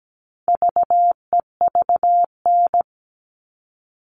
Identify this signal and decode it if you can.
Morse code, heard as a keyed tone, about 17 wpm, 700 Hz. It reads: VEVN
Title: All Cut Numbers - 4-Digit 17wpm